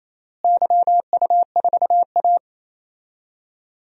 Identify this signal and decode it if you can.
Morse code, heard as a keyed tone, about 28 wpm, 700 Hz. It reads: YU4A
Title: DXCC-100 Call Signs - 4 Characters - Repeated 3x - 28wpm